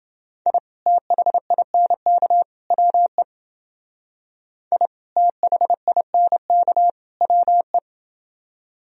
Morse code, heard as a keyed tone, about 30 wpm, 700 Hz. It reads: I THINK WE I THINK WE